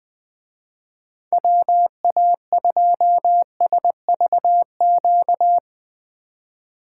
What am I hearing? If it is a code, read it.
Morse code, heard as a keyed tone, about 20 wpm, 700 Hz. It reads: WA2SVQ
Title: Speed Racing - USA 2x3 Call Signs 17wpm